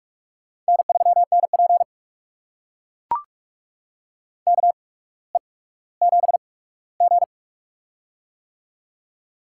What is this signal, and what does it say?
Morse code, heard as a keyed tone, about 44 wpm, spaced as for 19 wpm, 700 Hz. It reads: N3NP KE7G
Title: USA Call Signs - 4 Characters - 8x Character Spacing - 45wpm